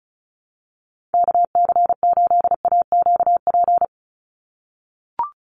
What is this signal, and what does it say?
Morse code, heard as a keyed tone, about 34 wpm, 700 Hz. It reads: KC8AQP
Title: USA Call Signs 35wpm